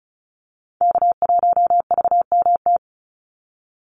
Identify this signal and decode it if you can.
Morse code, heard as a keyed tone, about 34 wpm, 700 Hz. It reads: K1VMT
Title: Call Signs Repeated 2x - 35wpm